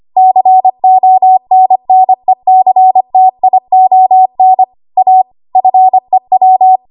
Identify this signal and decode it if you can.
Morse code, heard as a keyed tone, about 25 wpm, 750 Hz. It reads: CONNECTION A FEW
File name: Archive / 25 WPM / 25WPM